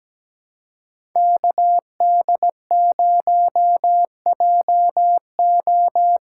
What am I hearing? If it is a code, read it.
Morse code, heard as a keyed tone, about 17 wpm, 700 Hz. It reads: KD0JO
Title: USA Call Signs - 6 Characters - 6x Character Spacing - 17wpm